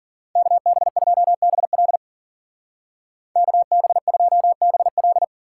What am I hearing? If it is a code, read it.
Morse code, heard as a keyed tone, about 50 wpm, 700 Hz. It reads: KB2BL KB2BL
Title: Speed Racing - USA Call Signs 40wpm